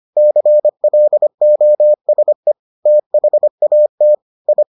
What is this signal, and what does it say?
Morse code, heard as a keyed tone, about 25 wpm, 600 Hz. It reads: CLOSE THAT I